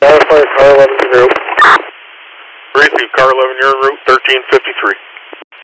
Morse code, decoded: none

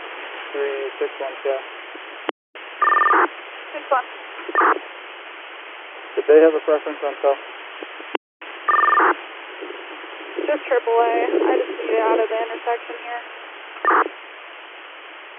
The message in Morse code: none

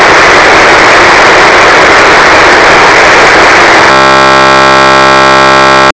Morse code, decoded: none

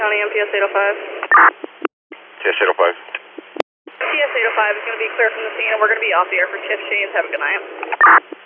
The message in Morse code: none

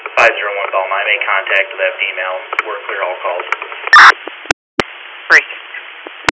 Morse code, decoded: none

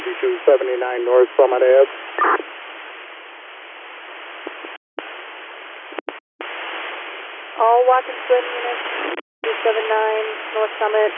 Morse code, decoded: none

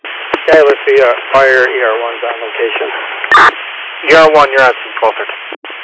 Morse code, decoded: none

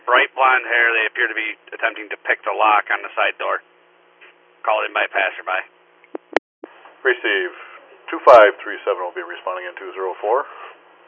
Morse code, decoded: none

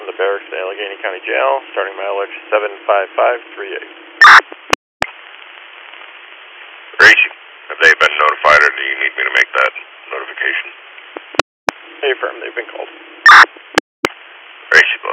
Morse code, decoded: none

